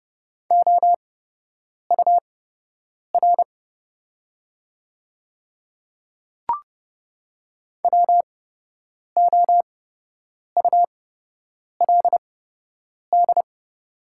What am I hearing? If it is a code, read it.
Morse code, heard as a keyed tone, about 30 wpm, spaced as for 8 wpm, 700 Hz. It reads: OUR WOULD
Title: 8x Character Spacing - Top 200 Words - 30wpm